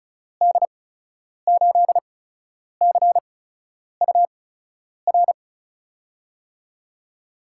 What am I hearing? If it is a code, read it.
Morse code, heard as a keyed tone, about 36 wpm, spaced as for 10 wpm, 700 Hz. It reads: D8CUR
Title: USA Call Signs - 6 Characters - 8x Character Spacing - 35wpm